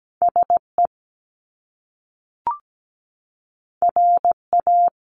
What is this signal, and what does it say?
Morse code, heard as a keyed tone, about 17 wpm, 700 Hz. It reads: SE RA
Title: Top Two Letter Combinations in Words 17wpm